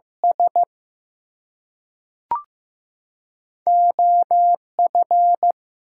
Morse code, heard as a keyed tone, about 15 wpm, spaced as for 3 wpm, 700 Hz. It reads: SO F
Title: Two Letter Words - 15wpm